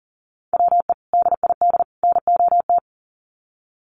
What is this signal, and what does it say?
Morse code, heard as a keyed tone, about 40 wpm, 700 Hz. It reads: WE DID NOT
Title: Sets of 3 Words 40wpm